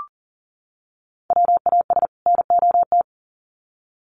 Morse code, heard as a keyed tone, about 40 wpm, 700 Hz. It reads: WAS NOT
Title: Sets of 2 Words 40wpm